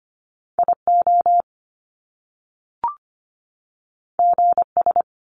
Morse code, heard as a keyed tone, about 25 wpm, 700 Hz. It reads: IO GS
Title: Top Two Letter Combinations in Words 25wpm